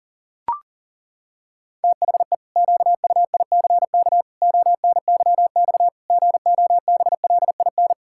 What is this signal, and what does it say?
Morse code, heard as a keyed tone, about 40 wpm, 700 Hz. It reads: THE QUICK ONYX GOBLIN
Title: Pangram Challenge v2 - 40wpm